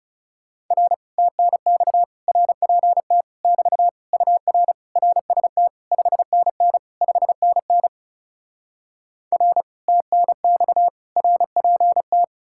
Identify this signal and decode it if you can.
Morse code, heard as a keyed tone, about 37 wpm, 700 Hz. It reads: R TNX RPT <BT> UR RST 5NN 5NN R TNX RPT
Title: Speed-Racing Common QSO Phrases - 30wpm